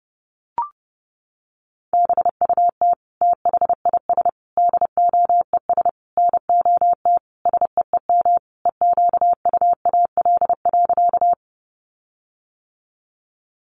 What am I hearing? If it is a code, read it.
Morse code, heard as a keyed tone, about 30 wpm, 700 Hz. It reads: BUT THIS DOES NOT SEEM EQUAL.
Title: Sentences from Top 1400 Words 30wpm